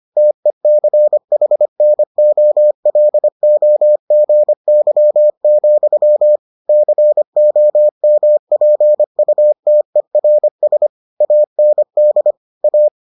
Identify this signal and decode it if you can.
Morse code, heard as a keyed tone, about 25 wpm, 600 Hz. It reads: TECHNOLOGY, COMPUTERS AND A